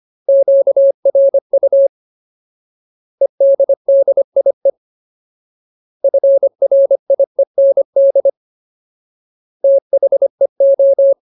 Morse code, heard as a keyed tone, about 25 wpm, 550 Hz. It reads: QRU EDDIE FRIEND THEO
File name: Ultimate Word List / 25WPM Ultimate